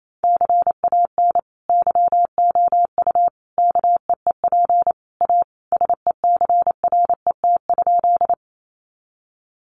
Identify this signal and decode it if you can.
Morse code, heard as a keyed tone, about 28 wpm, 700 Hz. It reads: CAN YOU KEEP A SECRET?
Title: Sentences from Top 1200 Words 28wpm